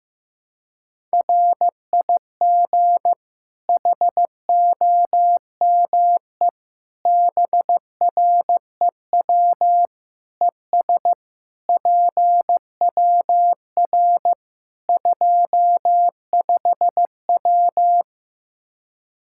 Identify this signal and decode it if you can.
Morse code, heard as a keyed tone, about 15 wpm, 700 Hz. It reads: RIG HOME BREW ES PWR 25W
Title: Practice QSO - Exchange 2 - Info - 15wpm